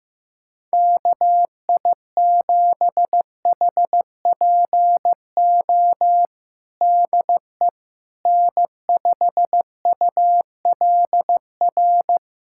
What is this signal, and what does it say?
Morse code, heard as a keyed tone, about 15 wpm, 700 Hz. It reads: KI7HPO DE N5ULR
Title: USA Callsign DE Callsign 15wpm